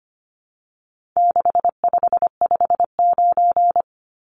Morse code, heard as a keyed tone, about 25 wpm, 700 Hz. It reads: 6559
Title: Numbers - 4-Digit 25wpm